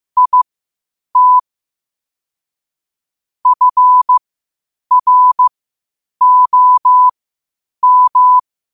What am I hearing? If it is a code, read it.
Morse code, heard as a keyed tone, about 15 wpm, spaced as for 8 wpm, 1000 Hz. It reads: IT FROM
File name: qotdfiles / 8wpm